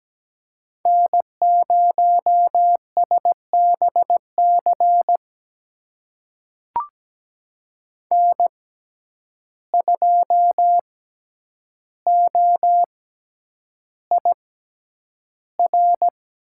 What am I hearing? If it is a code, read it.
Morse code, heard as a keyed tone, about 17 wpm, spaced as for 9 wpm, 700 Hz. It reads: N0SBC N2OIR
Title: USA Call Signs - 5 Characters - 6x Character Spacing - 17wpm